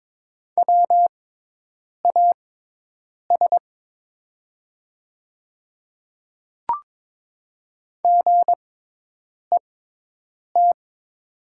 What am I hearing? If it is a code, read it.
Morse code, heard as a keyed tone, about 22 wpm, spaced as for 8 wpm, 700 Hz. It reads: WAS GET
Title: 6x Character Spacing - Top 200 Words - 22wpm